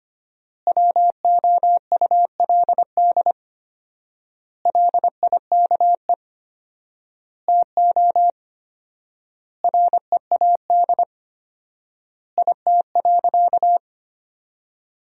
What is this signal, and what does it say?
Morse code, heard as a keyed tone, about 25 wpm, 700 Hz. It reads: WOULD LIKE TO READ IT.
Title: 4x Word-Spacing - Sentences from Top 500 Words - 25wpm